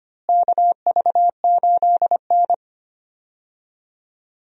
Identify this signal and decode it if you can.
Morse code, heard as a keyed tone, about 25 wpm, 700 Hz. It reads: KV8N
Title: USA Call Signs - 4 Characters - Repeated 3x - 25wpm